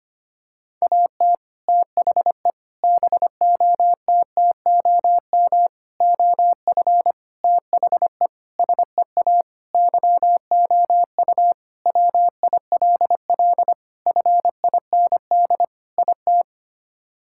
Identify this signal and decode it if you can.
Morse code, heard as a keyed tone, about 25 wpm, 700 Hz. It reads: AT THE BOTTOM OF THE SEA YOU WILL FIND IT
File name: practice / 25wpm / Sentences with 3-Word Phrases - 25wpm